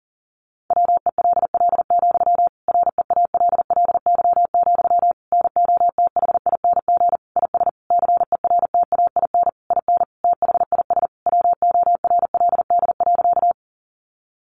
Morse code, heard as a keyed tone, about 40 wpm, 700 Hz. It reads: WELL, REALLY, NOTHING IS CERTAIN IN THIS WORLD.
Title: Sentences from Top 600 Words 40wpm